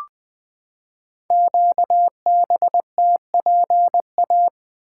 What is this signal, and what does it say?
Morse code, heard as a keyed tone, about 20 wpm, 700 Hz. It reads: QBTPA
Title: Code Groups - 5 Letters - 20wpm